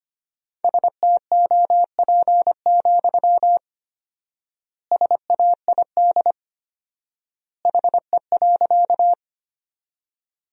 Morse code, heard as a keyed tone, about 25 wpm, 700 Hz. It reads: STOP, SAID HE.
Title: 4x Word-Spacing - Sentences from Top 300 Words - 25wpm